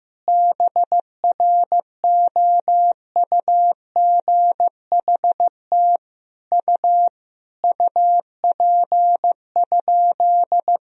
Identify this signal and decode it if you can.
Morse code, heard as a keyed tone, about 15 wpm, 700 Hz. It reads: BROUGHT U UP?
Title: Sentences from Top 500 Words and Abbreviations 15wpm